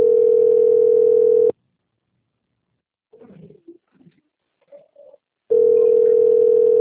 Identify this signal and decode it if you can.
Ringback tone (North America), 440 + 480 Hz.